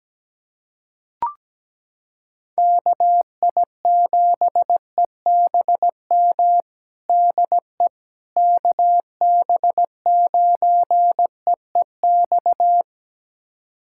Morse code, heard as a keyed tone, about 17 wpm, 700 Hz. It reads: KI7EBM DE KB9EEX
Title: USA Callsign DE Callsign - 6 Characters - 17wpm